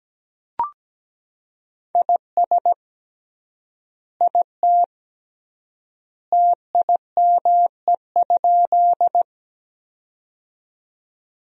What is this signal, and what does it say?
Morse code, heard as a keyed tone, about 17 wpm, 700 Hz. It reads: IS IT TIME?
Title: 3x Word-Spacing - Sentences from Top 100 Words - 17wpm